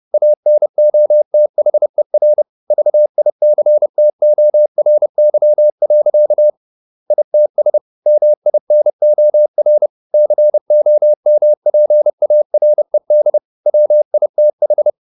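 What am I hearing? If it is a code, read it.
Morse code, heard as a keyed tone, about 30 wpm, 600 Hz. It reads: ANOTHER VICTORY. ITS MINOR COMPARED WITH
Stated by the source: Morse code at 30 wpm, the novel Never by Ken Follett in English, from CW med Gnister.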